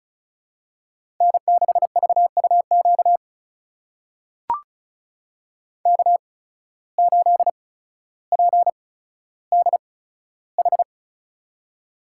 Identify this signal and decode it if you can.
Morse code, heard as a keyed tone, about 36 wpm, spaced as for 15 wpm, 700 Hz. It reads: N6VUQ K8PDH
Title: USA Call Signs - 5 Characters - 8x Character Spacing - 35wpm